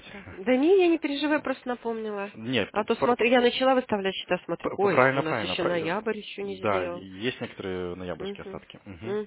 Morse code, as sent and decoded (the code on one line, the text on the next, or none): none